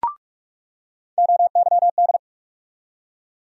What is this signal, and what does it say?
Morse code, heard as a keyed tone, about 46 wpm, 700 Hz. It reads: OYD
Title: USA Call Sign Postfixes - 3 Letters 45wpm